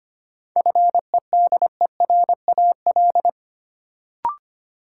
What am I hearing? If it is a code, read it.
Morse code, heard as a keyed tone, about 25 wpm, 700 Hz. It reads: FEDERAL
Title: Sets of 3 Words - New Word Drill - 25wpm